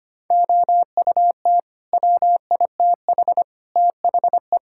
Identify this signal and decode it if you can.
Morse code, heard as a keyed tone, about 25 wpm, 700 Hz. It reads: OUT WITH THE
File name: Sets of 9 Words - 25wpm